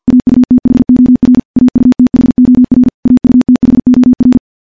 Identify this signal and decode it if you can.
Morse code, heard as a keyed tone, about 51 wpm, 250 Hz. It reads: AUTHOR AUTHOR AUTHOR
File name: morse / morse-training-50WPM